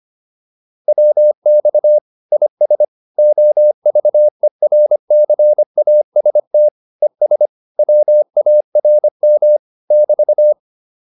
Morse code, heard as a keyed tone, about 25 wpm, 600 Hz. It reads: WX IS OVERCAST ES WARM <BT>